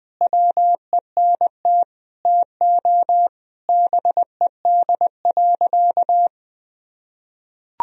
Morse code, heard as a keyed tone, about 21 wpm, 700 Hz. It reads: WENT TO BED.
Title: Sentences from Top 1000 Words - Review Part 3 - 20wpm